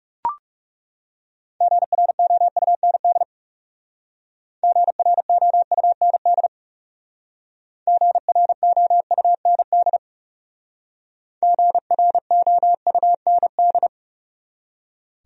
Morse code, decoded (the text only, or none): GROUND GROUND GROUND GROUND